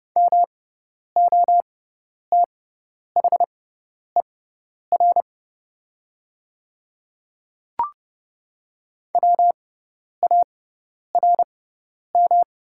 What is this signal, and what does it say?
Morse code, heard as a keyed tone, about 29 wpm, spaced as for 10 wpm, 690 Hz. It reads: MOTHER WARM
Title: 6x Character Spacing - Top 500 Words - 30wpm